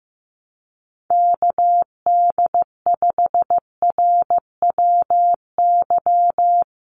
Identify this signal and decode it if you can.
Morse code, heard as a keyed tone, about 15 wpm, 700 Hz. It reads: KD5RWY
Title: USA Call Signs 15wpm